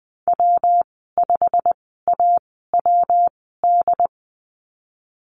Morse code, heard as a keyed tone, about 20 wpm, spaced as for 15 wpm, 700 Hz. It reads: W5AWD
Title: 147 Lesson 24g - Letter y - Call Signs 20wpm